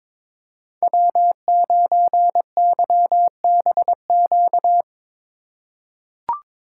Morse code, heard as a keyed tone, about 22 wpm, 700 Hz. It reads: W9YBQ
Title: USA Call Signs - 5 Characters - 4x Character Spacing - 22wpm